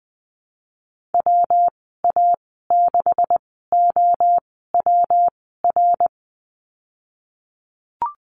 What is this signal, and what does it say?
Morse code, heard as a keyed tone, about 20 wpm, spaced as for 15 wpm, 700 Hz. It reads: WA6OWR
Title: 128 Lesson 21g - Number 6 - Call Signs 20wpm